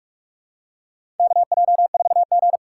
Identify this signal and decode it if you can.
Morse code, heard as a keyed tone, about 46 wpm, 700 Hz. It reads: KJ4G
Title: USA Call Signs - 4 Characters - 3x Character Spacing - 45wpm